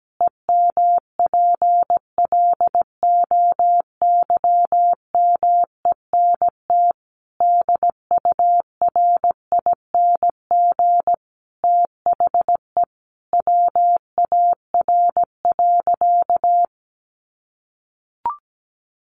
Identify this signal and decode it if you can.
Morse code, heard as a keyed tone, about 17 wpm, 700 Hz. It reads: EMPLOYMENT DURING THE WAR.
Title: Sentences from Top 1500 Words 17wpm